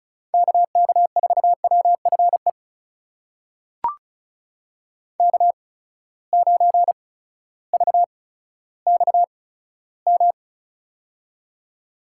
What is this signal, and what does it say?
Morse code, heard as a keyed tone, about 36 wpm, spaced as for 16 wpm, 700 Hz. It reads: KK4WFE K9VXM
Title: USA Call Signs - 8x Character Spacing - 35wpm